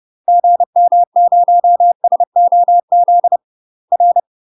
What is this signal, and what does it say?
Morse code, heard as a keyed tone, about 30 wpm, 700 Hz. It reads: GM0SOZ R
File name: UK MASTER QSOs TRAINING / 30WPM / UK MASTER QSOs 0044